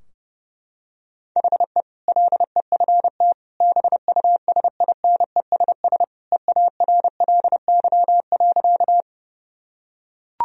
Morse code, heard as a keyed tone, about 30 wpm, 700 Hz. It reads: HE LEFT BUSINESS EARLY.
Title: Sentences from Top 1000 Words - Review Part 2 - 30wpm